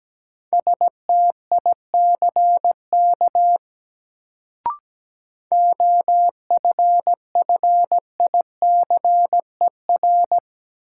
Morse code, heard as a keyed tone, about 17 wpm, 700 Hz. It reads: STICK OFFICER
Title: Sentences from Top 1200 Words - New Word Drill Alt - 17wpm